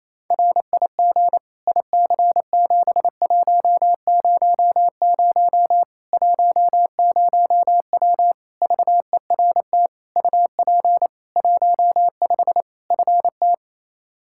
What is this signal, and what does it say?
Morse code, heard as a keyed tone, about 28 wpm, 700 Hz. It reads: RIG IC7100 10W VERT UP 15 FT
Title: Practice QSO - Exchange 2 - Info - 28wpm